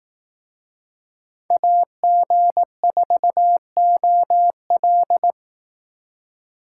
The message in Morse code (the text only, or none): AG4OL